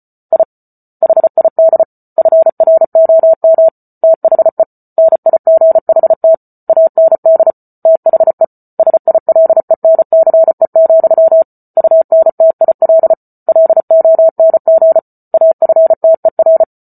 Morse code, heard as a keyed tone, about 34 wpm, 650 Hz. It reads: I HID FROM THE NIGHT AND THE SILENCE, UNTIL LONG AFTER